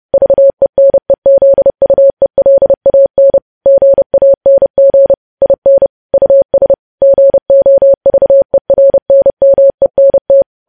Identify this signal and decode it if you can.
Morse code, heard as a keyed tone, about 30 wpm, 550 Hz. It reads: VENEZUELAN GANG IN US GOVERNMENT
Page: news headlines in morse code at 30 wpm